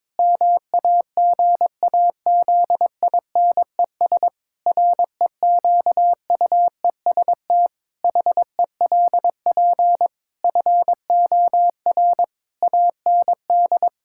Morse code, heard as a keyed tone, about 22 wpm, 700 Hz. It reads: MAGAZINES REQUEST HELP FOR AND
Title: Pangram Challenge v2 - 22wpm